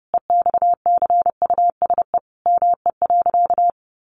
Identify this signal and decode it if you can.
Morse code, heard as a keyed tone, about 28 wpm, 700 Hz. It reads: EXCUSE ME.
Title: Sentences from Top 700 Words 30wpm